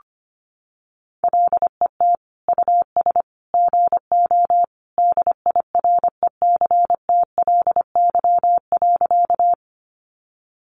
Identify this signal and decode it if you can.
Morse code, heard as a keyed tone, about 25 wpm, 700 Hz. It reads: LET US GO DIRECTLY.
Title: Sentences from Top 800 Words 25wpm